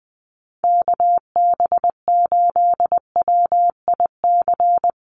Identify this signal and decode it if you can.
Morse code, heard as a keyed tone, about 20 wpm, 700 Hz. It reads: KB8WIC
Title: USA Call Signs Repeated 2x 20wpm